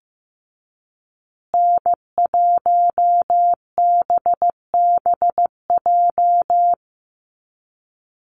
Morse code, heard as a keyed tone, about 15 wpm, 700 Hz. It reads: N1BBJ
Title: USA Call Signs Repeated 3x 15wpm